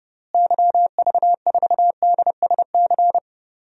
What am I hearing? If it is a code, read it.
Morse code, heard as a keyed tone, about 30 wpm, 700 Hz. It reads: YV4DSC
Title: DXCC-100 Call Signs - 6 Characters - Repeated 2x - 30wpm